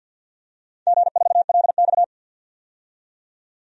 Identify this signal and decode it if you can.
Morse code, heard as a keyed tone, about 47 wpm, 700 Hz. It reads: G4LX
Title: DXCC-100 Call Signs - 4 Characters - 50wpm